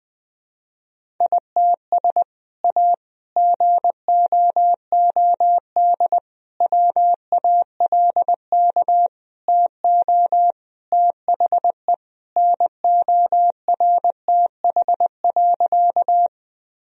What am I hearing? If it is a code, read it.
Morse code, heard as a keyed tone, about 20 wpm, 700 Hz. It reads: ITS A GOOD WALK TO THE NORTH.
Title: Sentences from Top 1000 Words - Review Part 1 - 20wpm